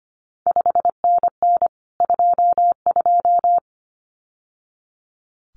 Morse code, heard as a keyed tone, about 25 wpm, 700 Hz. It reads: 5NN 22
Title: CQ WW DX Contest Exchange Encore - 25wpm